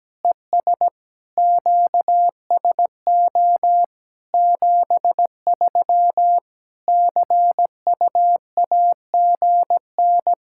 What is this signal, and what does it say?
Morse code, heard as a keyed tone, about 17 wpm, 700 Hz. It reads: ES QSO 73 CUAGN
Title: Practice QSO - Exchange 4 - Ending - 17wpm